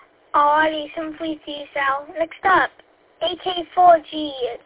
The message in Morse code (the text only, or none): none